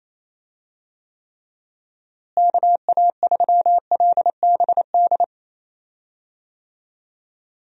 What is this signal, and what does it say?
Morse code, heard as a keyed tone, about 28 wpm, 700 Hz. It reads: KA3LBD
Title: USA Call Signs - 6 Characters - Repeated 2x - 28wpm